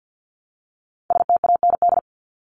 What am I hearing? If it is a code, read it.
Morse code, heard as a keyed tone, about 50 wpm, 700 Hz. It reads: STAND
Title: Top 300 Words Repeated 50wpm